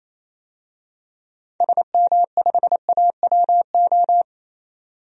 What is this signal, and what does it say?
Morse code, heard as a keyed tone, about 28 wpm, 700 Hz. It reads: SM5AWO
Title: DXCC-100 Call Signs - 6 Characters - 28wpm